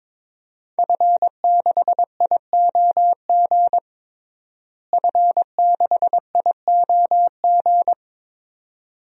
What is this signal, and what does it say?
Morse code, heard as a keyed tone, about 22 wpm, 700 Hz. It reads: F6IOG F6IOG
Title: DXCC-100 Call Signs - 5 Characters - Repeated 2x - 22wpm